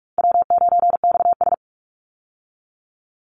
Morse code, heard as a keyed tone, about 44 wpm, 700 Hz. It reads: W9XS
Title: Call Signs Repeated 2x - 45wpm